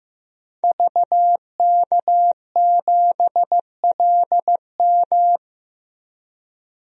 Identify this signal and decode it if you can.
Morse code, heard as a keyed tone, about 15 wpm, 700 Hz. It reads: VK7LM
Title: DXCC-100 Call Signs - 5 Characters - 15wpm